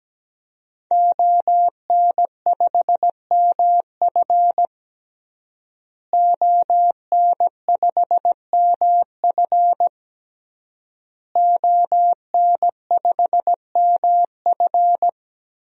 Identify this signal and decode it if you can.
Morse code, heard as a keyed tone, about 17 wpm, 700 Hz. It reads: ON5MF ON5MF ON5MF
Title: DXCC-100 Call Signs - 5 Characters - Repeated 3x - 17wpm